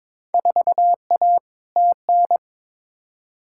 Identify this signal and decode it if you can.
Morse code, heard as a keyed tone, about 22 wpm, 700 Hz. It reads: 4A TN
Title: Field Day Exchanges - 22wpm